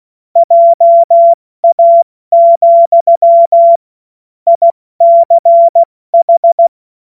Morse code, heard as a keyed tone, about 16 wpm, 680 Hz. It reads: JA, ICH
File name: Morsekurs / Witze / Witz 08. November 2022 70 F80